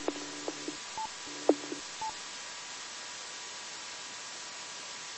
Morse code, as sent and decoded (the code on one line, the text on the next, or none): none